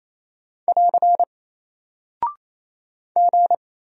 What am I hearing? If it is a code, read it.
Morse code, heard as a keyed tone, about 28 wpm, 700 Hz. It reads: <AR> G
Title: Single Character (Letter:Number:Punctuation:Prosign) Alt 28wpm